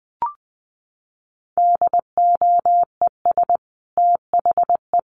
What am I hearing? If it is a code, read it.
Morse code, heard as a keyed tone, about 20 wpm, 700 Hz. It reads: DOES THE
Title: Sentences from Top 1500 Words 20wpm